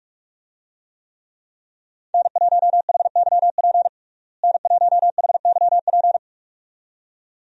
Morse code, heard as a keyed tone, about 43 wpm, 700 Hz. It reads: N1HYP N1HYP
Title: USA Call Signs - 5 Characters - Repeated 2x - 45wpm